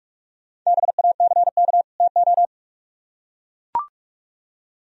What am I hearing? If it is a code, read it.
Morse code, heard as a keyed tone, about 43 wpm, 700 Hz. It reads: BACK TO